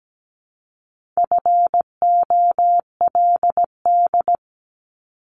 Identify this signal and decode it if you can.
Morse code, heard as a keyed tone, about 17 wpm, 700 Hz. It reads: FOLD
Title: Multi-letter Phonemes Repeated 3x 17wpm